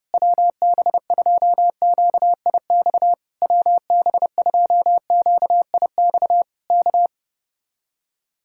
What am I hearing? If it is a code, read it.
Morse code, heard as a keyed tone, about 35 wpm, 700 Hz. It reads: WB2QIX WB2QIX K
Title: Speed-Racing Common QSO Phrases - 30wpm